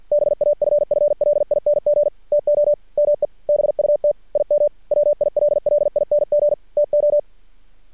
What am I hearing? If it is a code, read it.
Morse code, heard as a keyed tone, about 50 wpm, 590 Hz. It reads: BAFFLING TO ME BUT IM WILLING TO
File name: archived / CRSnet